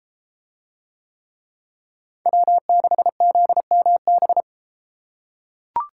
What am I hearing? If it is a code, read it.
Morse code, heard as a keyed tone, about 32 wpm, 700 Hz. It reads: W6ZMB